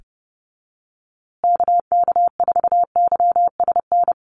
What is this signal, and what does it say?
Morse code, heard as a keyed tone, about 30 wpm, 700 Hz. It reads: KK4YSN
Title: USA Call Signs Repeated 2x 30wpm